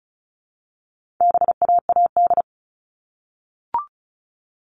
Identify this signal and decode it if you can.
Morse code, heard as a keyed tone, about 34 wpm, 700 Hz. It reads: BAAD
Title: All Cut Numbers - 4-Digit 35wpm